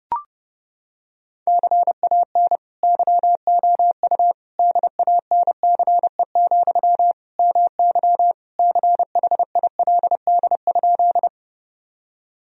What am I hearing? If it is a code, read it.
Morse code, heard as a keyed tone, about 30 wpm, 700 Hz. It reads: CAN YOU DANCE, MY CHILD?
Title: Sentences from Top 1000 Words - Review Part 4 - 30wpm